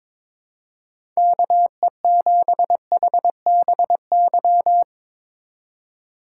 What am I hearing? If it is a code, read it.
Morse code, heard as a keyed tone, about 22 wpm, 700 Hz. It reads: KE7HBY
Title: USA Call Signs - 2x Character Spacing - 22wpm